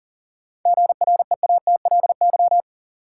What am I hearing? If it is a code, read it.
Morse code, heard as a keyed tone, about 40 wpm, 700 Hz. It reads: GREATLY
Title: Sentences from Top 1400 Words - New Word Drill - 40wpm